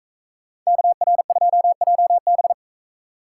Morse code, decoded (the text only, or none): KR2JB